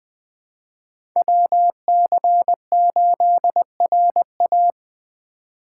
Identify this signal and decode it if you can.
Morse code, heard as a keyed tone, about 20 wpm, 700 Hz. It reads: WC8RA